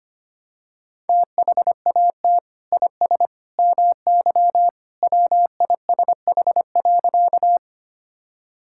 Morse code, heard as a keyed tone, about 25 wpm, 700 Hz. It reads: THAT IS MY WISH.
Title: Sentences from Top 1600 Words 25wpm